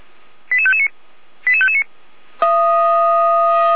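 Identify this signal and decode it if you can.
Five-tone selective call (ZVEI), heard as quick runs of five tones, tones of about 70 ms. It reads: 88559 88559
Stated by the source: ZVEI five-tone selective call received by SDR.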